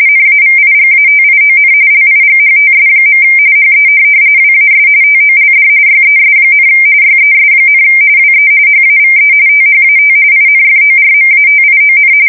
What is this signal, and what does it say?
RTTY, about 45.45 baud. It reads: NAVIGATION LIGHTS. HE WAS TOLD YES, YES, WE HAVE THAT INFORMATION
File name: fsk170 / USS-New-Jersey